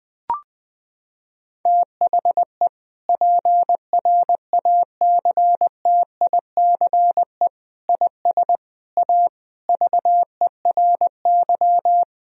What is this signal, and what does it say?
Morse code, heard as a keyed tone, about 21 wpm, 700 Hz. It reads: THE PRACTICE IS A VERY
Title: Sentences from Top 1000 Words - Review Part 4 - 20wpm